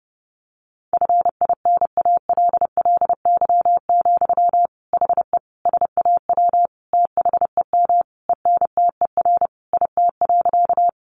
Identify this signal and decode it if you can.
Morse code, heard as a keyed tone, about 30 wpm, 700 Hz. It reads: FINALLY, HE SAW THEM ENTER IT.
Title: Sentences from Top 1300 Words 30wpm